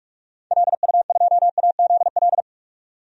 Morse code, heard as a keyed tone, about 43 wpm, 700 Hz. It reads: LU2AZL
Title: DXCC-100 Call Signs - 6 Characters - 45wpm